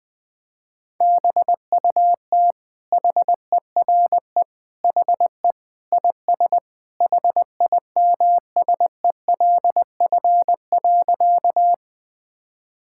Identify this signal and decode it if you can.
Morse code, heard as a keyed tone, about 20 wpm, 700 Hz. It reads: BUT HERE HE IS HIMSELF.
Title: Sentences from Top 1600 Words 20wpm